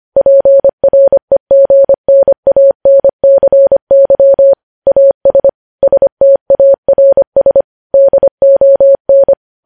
Morse code, heard as a keyed tone, about 25 wpm, 550 Hz. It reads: PREGNANCY AS STARS DON
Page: news-headlines-in-morse-code-at-25-wpm